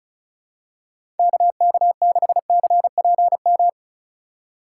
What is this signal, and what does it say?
Morse code, heard as a keyed tone, about 36 wpm, 700 Hz. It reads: KK6CPM
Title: USA Call Signs - 3x Character Spacing - 35wpm